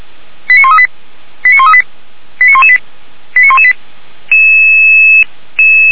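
Five-tone selective call (ZVEI), heard as quick runs of five tones, tones of about 70 ms. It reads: 89138 89138 89118 89118
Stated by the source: ZVEI five-tone selective call 89227 received by SDR, from Feuerwehr Orlamünde audio server.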